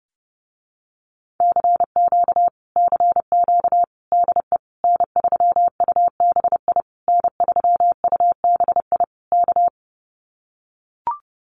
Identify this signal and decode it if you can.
Morse code, heard as a keyed tone, about 30 wpm, 700 Hz. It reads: CQ CQ DE N3UBI N3UBI K
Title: Common QSO Phrases Encore 30wpm